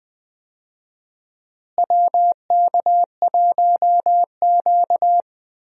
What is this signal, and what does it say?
Morse code, heard as a keyed tone, about 20 wpm, 700 Hz. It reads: WK1Q
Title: USA Call Signs - 4 Characters - Repeated 2x - 20wpm